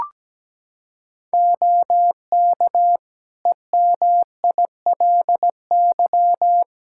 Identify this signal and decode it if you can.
Morse code, heard as a keyed tone, about 17 wpm, 700 Hz. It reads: OK EMILY
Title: Practice QSO - Exchange 4 - Ending - 17wpm